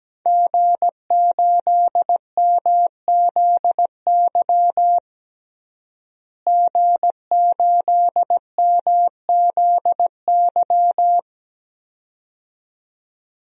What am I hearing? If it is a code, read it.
Morse code, heard as a keyed tone, about 17 wpm, 700 Hz. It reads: G8MZY G8MZY
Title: DXCC-100 Call Signs - 5 Characters - Repeated 2x - 17wpm